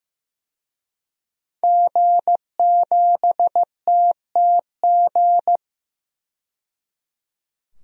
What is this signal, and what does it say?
Morse code, heard as a keyed tone, about 15 wpm, 700 Hz. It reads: G7TTG
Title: DXCC-100 Call Signs - 5 Characters - 15wpm